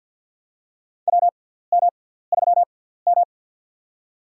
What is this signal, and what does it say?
Morse code, heard as a keyed tone, about 51 wpm, spaced as for 17 wpm, 700 Hz. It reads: WM3M
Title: USA Call Signs - 6x Character Spacing - 50wpm